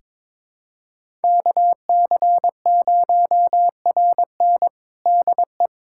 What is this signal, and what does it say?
Morse code, heard as a keyed tone, about 21 wpm, 700 Hz. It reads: KC0RN DE
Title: USA Callsign DE Callsign 22wpm